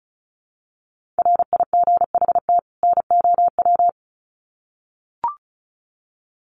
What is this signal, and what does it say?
Morse code, heard as a keyed tone, about 34 wpm, 700 Hz. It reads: RIGHT NOW
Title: Sets of 2 Words 35wpm